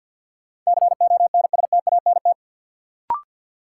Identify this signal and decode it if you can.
Morse code, heard as a keyed tone, about 51 wpm, 700 Hz. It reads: CONSTANT
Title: Sentences from Top 1500 Words - New Word Drill Alt - 50wpm